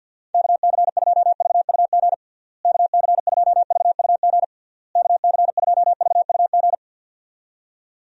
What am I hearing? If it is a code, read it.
Morse code, heard as a keyed tone, about 47 wpm, 700 Hz. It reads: KC2VUG KC2VUG KC2VUG
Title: USA Call Signs - 6 Characters - Repeated 3x - 50wpm